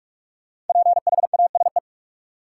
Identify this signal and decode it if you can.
Morse code, heard as a keyed tone, about 43 wpm, 700 Hz. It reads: PHASE
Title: Sentences from Top 1200 Words - New Word Drill - 45wpm